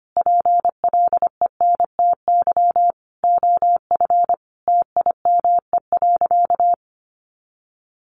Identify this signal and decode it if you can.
Morse code, heard as a keyed tone, about 25 wpm, 700 Hz. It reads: PLENTY OF TIME.
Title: Sentences from Top 1300 Words 25wpm